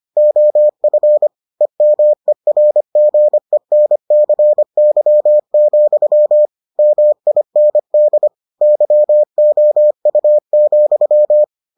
Morse code, heard as a keyed tone, about 25 wpm, 600 Hz. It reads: OF EMERGENCY, MIND YOU,